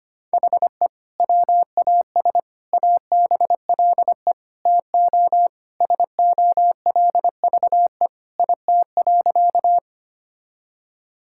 Morse code, heard as a keyed tone, about 25 wpm, 700 Hz. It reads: HE WAS ABLE TO SOLVE IT.
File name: practice / 25wpm / Sentences with 3-Word Phrases - 25wpm